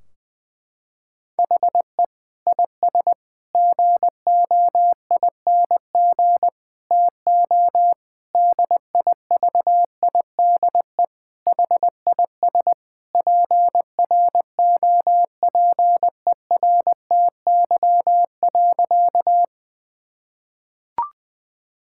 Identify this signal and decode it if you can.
Morse code, heard as a keyed tone, about 20 wpm, 700 Hz. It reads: HE IS GOING TO DIVIDE HIS PROPERTY.
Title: Sentences from Top 1600 Words 20wpm